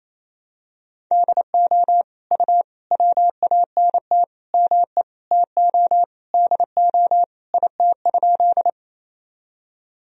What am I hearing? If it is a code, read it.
Morse code, heard as a keyed tone, about 28 wpm, 700 Hz. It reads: DO U WANT ME TO DO IT?
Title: Sentences from Top 200 Words and Abbreviations 28wpm